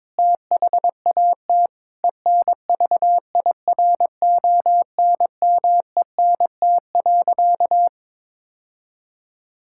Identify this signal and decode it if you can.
Morse code, heard as a keyed tone, about 22 wpm, 700 Hz. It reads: THAT ENVIRONMENT.
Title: Sentences from Top 1000 Words - Review Part 5 - 22wpm